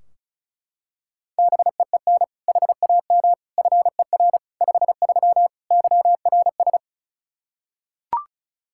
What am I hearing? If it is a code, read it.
Morse code, heard as a keyed tone, about 36 wpm, 700 Hz. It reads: BEEN HAM FER 53 YRS
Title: Practice QSO - Exchange 3 - More Info - 35wpm